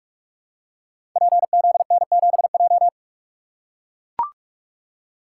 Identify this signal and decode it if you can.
Morse code, heard as a keyed tone, about 44 wpm, 700 Hz. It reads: PZN7J
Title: Code Groups - 5 Letter:Numbers - 45wpm